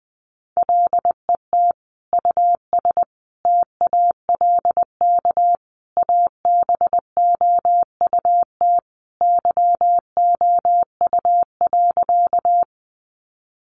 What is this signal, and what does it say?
Morse code, heard as a keyed tone, about 20 wpm, 700 Hz. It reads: LET US TALK ABOUT YOU.
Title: Sentences from Top 400 Words 20wpm